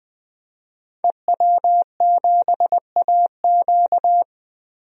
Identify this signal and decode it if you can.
Morse code, heard as a keyed tone, about 20 wpm, 700 Hz. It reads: EW7AQ
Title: DXCC-100 Call Signs - 5 Characters - 20wpm